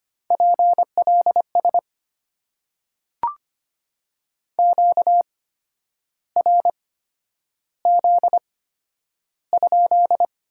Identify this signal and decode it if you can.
Morse code, heard as a keyed tone, about 25 wpm, spaced as for 10 wpm, 700 Hz. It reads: PLS QRZ?
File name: Top 90 QSO Elements - 8x Character Spacing - 25wpm